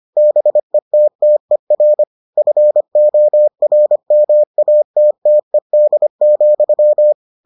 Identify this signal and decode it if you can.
Morse code, heard as a keyed tone, about 25 wpm, 600 Hz. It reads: BETTER FORMATTED,